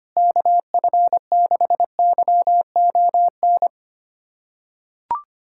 Morse code, heard as a keyed tone, about 25 wpm, 700 Hz. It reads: KF6YON
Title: USA Callsign DE Callsign 25wpm